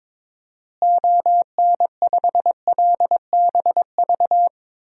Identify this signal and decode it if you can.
Morse code, heard as a keyed tone, about 22 wpm, 700 Hz. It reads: ON5LBV
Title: DXCC-100 Call Signs - 6 Characters - Repeated 2x - 22wpm